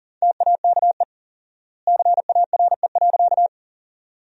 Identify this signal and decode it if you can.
Morse code, heard as a keyed tone, about 40 wpm, 700 Hz. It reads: TAKE CARE.
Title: 4x Word-Spacing - Sentences from Top 300 Words - 40wpm